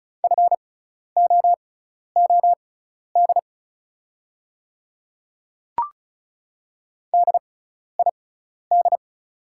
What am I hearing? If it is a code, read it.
Morse code, heard as a keyed tone, about 36 wpm, spaced as for 12 wpm, 690 Hz. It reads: FOOD DID
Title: 6x Character Spacing - Top 300 Words - 35wpm